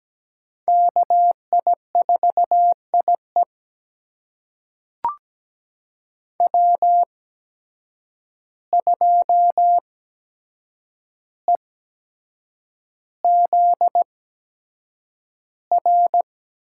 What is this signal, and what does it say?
Morse code, heard as a keyed tone, about 17 wpm, spaced as for 7 wpm, 700 Hz. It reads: KI4IE W2EZR
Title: USA Call Signs - 5 Characters - 8x Character Spacing - 17wpm